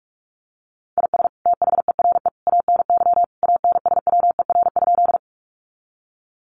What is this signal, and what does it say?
Morse code, heard as a keyed tone, about 44 wpm, 700 Hz. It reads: IS THERE ANY ANSWER?
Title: Sentences from Top 300 Words 45wpm